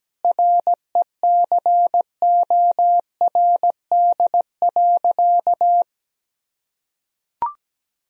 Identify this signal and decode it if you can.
Morse code, heard as a keyed tone, about 17 wpm, 700 Hz. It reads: RECORD.
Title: Sentences from Top 1000 Words - Review Part 3 - 17wpm